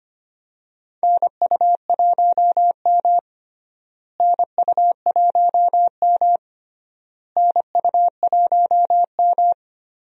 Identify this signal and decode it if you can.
Morse code, heard as a keyed tone, about 25 wpm, 700 Hz. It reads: NU1M NU1M NU1M
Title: USA Call Signs - 4 Characters - Repeated 3x - 25wpm